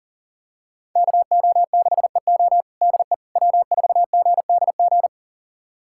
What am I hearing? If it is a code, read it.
Morse code, heard as a keyed tone, about 40 wpm, 700 Hz. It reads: KO6EO DE W4GDG
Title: USA Callsign DE Callsign - 5 Characters - 40wpm